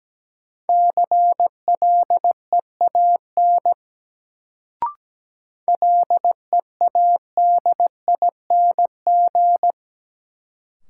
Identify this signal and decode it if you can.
Morse code, heard as a keyed tone, about 17 wpm, 700 Hz. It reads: CLEAN LEADING
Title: Sentences from Top 1100 Words - New Word Drill Alt - 17wpm